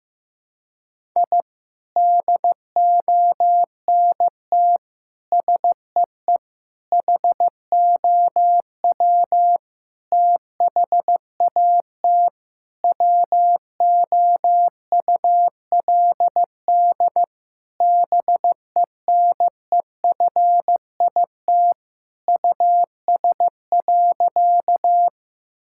Morse code, heard as a keyed tone, about 15 wpm, 700 Hz. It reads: I DONT SEE HOW THAT WOULD BENEFIT US.
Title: Sentences from Top 1000 Words - Review Part 6 - 15wpm